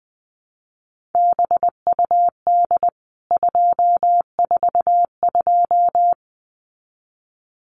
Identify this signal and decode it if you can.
Morse code, heard as a keyed tone, about 20 wpm, 700 Hz. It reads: BUD 242
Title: CWT Exchange 20wpm